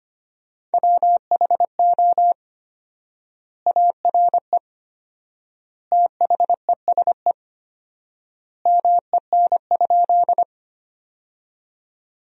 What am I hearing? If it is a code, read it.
Morse code, heard as a keyed tone, about 25 wpm, 700 Hz. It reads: WHO ARE THESE MEN?
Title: 4x Word-Spacing - Sentences from Top 200 Words - 25wpm